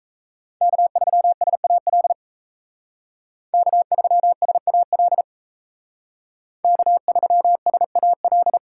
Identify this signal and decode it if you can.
Morse code, heard as a keyed tone, about 35 wpm, 700 Hz. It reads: K3SAL K3SAL K3SAL
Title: Speed Racing - USA 1x3 Call Signs 28wpm